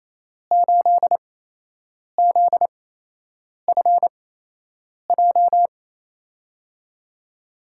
Morse code, heard as a keyed tone, about 27 wpm, spaced as for 8 wpm, 700 Hz. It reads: 8ZFJ
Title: USA Call Signs - 6 Characters - 8x Character Spacing - 28wpm